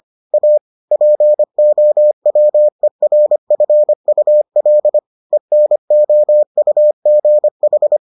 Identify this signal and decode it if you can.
Morse code, heard as a keyed tone, about 25 wpm, 600 Hz. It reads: A POWERFUL ENOUGH